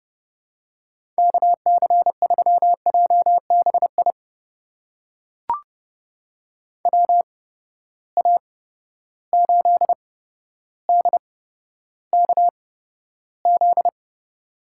Morse code, heard as a keyed tone, about 30 wpm, spaced as for 13 wpm, 700 Hz. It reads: KC3JBI WA8DKZ
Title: USA Call Signs - 6 Characters - 8x Character Spacing - 30wpm